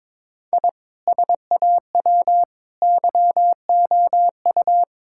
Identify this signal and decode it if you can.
Morse code, heard as a keyed tone, about 22 wpm, 700 Hz. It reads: I SAW YOU
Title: Sentences from Top 1000 Words - Review Part 5 - 22wpm